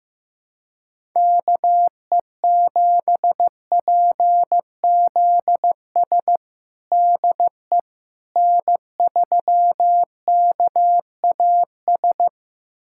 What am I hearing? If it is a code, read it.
Morse code, heard as a keyed tone, about 15 wpm, 700 Hz. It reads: KE7PZS DE N3KAS
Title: USA Callsign DE Callsign 15wpm